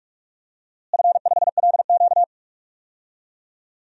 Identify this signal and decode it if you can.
Morse code, heard as a keyed tone, about 41 wpm, 700 Hz. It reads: F5LQ
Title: DXCC-100 Call Signs - 4 Characters - Repeated 3x - 45wpm